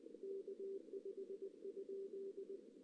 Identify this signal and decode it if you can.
Morse code, heard as a keyed tone, about 20 wpm, 400 Hz. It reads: K5?
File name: 2016 JAN CW NAQP / 2016 Jan naqp k5ot - 0-3